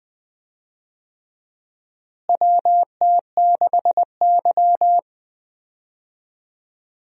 Morse code, heard as a keyed tone, about 20 wpm, 700 Hz. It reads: WT6Y
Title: USA Call Signs - 4 Characters - Repeated 2x - 20wpm